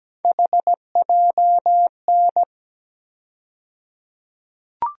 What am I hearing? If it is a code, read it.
Morse code, heard as a keyed tone, about 17 wpm, 700 Hz. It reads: HJN
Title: USA Call Sign Postfixes - 3 Letters 17wpm